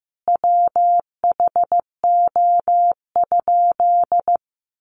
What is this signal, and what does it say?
Morse code, heard as a keyed tone, about 15 wpm, 700 Hz. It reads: WHO?
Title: Sentences from Top 800 Words 15wpm